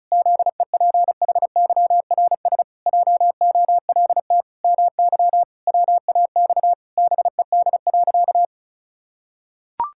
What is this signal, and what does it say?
Morse code, heard as a keyed tone, about 34 wpm, 700 Hz. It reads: ZEPHYRS JOLT MY WAX BED.
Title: Pangram Challenge v2 - 35wpm